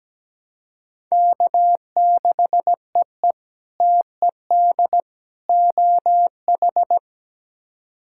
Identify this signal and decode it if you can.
Morse code, heard as a keyed tone, about 17 wpm, 700 Hz. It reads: K6EE TED OH
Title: K1USN SST QSO Practice - 17wpm